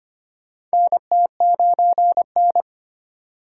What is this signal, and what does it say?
Morse code, heard as a keyed tone, about 25 wpm, 700 Hz. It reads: NT9N
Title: USA Call Signs - 3x Character Spacing - 25wpm